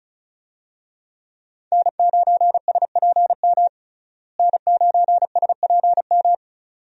Morse code, heard as a keyed tone, about 36 wpm, 700 Hz. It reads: N9SPM N9SPM
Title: USA Call Signs - 5 Characters - Repeated 2x - 35wpm